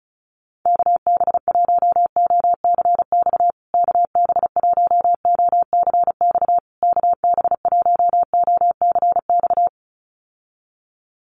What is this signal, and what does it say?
Morse code, heard as a keyed tone, about 34 wpm, 700 Hz. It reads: KB1OCX KB1OCX KB1OCX
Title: USA Call Signs Repeated 3x 35wpm